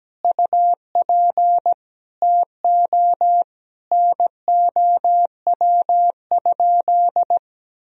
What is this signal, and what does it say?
Morse code, heard as a keyed tone, about 17 wpm, 700 Hz. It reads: UP TO NOW?
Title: Sentences from Top 100 Words and Abbreviations 17wpm